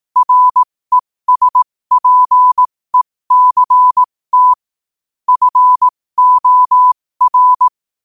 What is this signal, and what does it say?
Morse code, heard as a keyed tone, about 18 wpm, spaced as for 15 wpm, 1000 Hz. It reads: RESPECT FOR